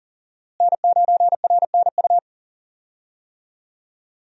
Morse code, heard as a keyed tone, about 40 wpm, 700 Hz. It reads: N9RNU
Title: USA Call Signs - 5 Characters - Repeated 2x - 40wpm